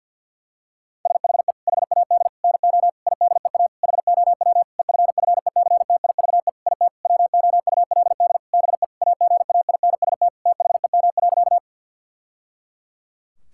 Morse code, heard as a keyed tone, about 52 wpm, 700 Hz. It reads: SHE HAD NO IDEA HOW EFFECTIVE IT WOULD BE AGAINST THEM.
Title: Sentences from Top 1000 Words - Review Part 6 - 50wpm